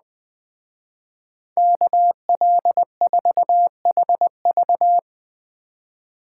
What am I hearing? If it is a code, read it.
Morse code, heard as a keyed tone, about 20 wpm, 700 Hz. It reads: KL4HV